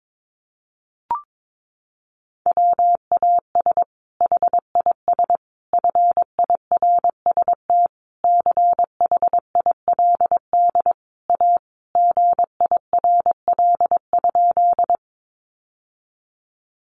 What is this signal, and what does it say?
Morse code, heard as a keyed tone, about 22 wpm, 700 Hz. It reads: WAS HIS FIRST CHILD A GIRL?
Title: Sentences from Top 800 Words 22wpm